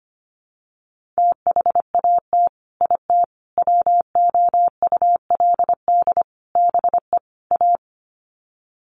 Morse code, heard as a keyed tone, about 25 wpm, 700 Hz. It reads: THAT IT WOULD BE A
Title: Sets of 5 Words Encore2 25wpm